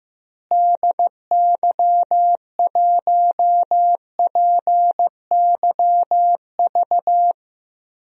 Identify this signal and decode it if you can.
Morse code, heard as a keyed tone, about 15 wpm, 700 Hz. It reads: DY1PYV
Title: DXCC-100 Call Signs - 6 Characters - Repeated 3x - 15wpm